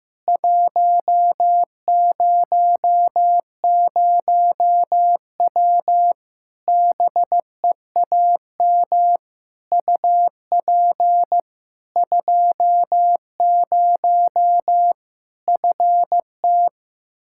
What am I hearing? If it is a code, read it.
Morse code, heard as a keyed tone, about 15 wpm, 700 Hz. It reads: 100W BEAM UP 20 FT
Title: K1USN SST QSO Practice 15wpm